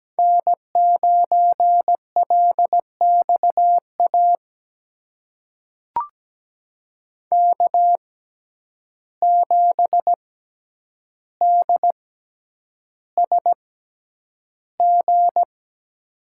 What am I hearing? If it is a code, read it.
Morse code, heard as a keyed tone, about 17 wpm, spaced as for 9 wpm, 700 Hz. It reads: N9LXA K7DSG
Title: USA Call Signs - 5 Characters - 6x Character Spacing - 17wpm